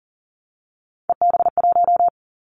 Morse code, heard as a keyed tone, about 38 wpm, 700 Hz. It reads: EB1
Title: Callsign Prefixes for Top 25 DXCC - 40wpm